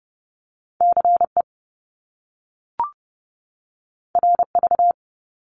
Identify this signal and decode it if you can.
Morse code, heard as a keyed tone, about 30 wpm, 700 Hz. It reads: CE RV
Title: Top Two Letter Combinations in Words 30wpm